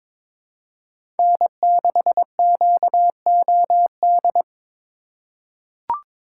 Morse code, heard as a keyed tone, about 22 wpm, 700 Hz. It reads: N6QOD
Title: USA Call Signs - 3x Character Spacing - 22wpm